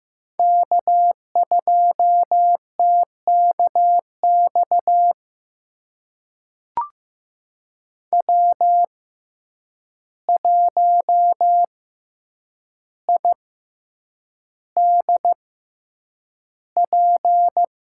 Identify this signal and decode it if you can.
Morse code, heard as a keyed tone, about 15 wpm, spaced as for 8 wpm, 700 Hz. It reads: K2TKX W1IDP
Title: USA Call Signs - 5 Characters - 6x Character Spacing - 15wpm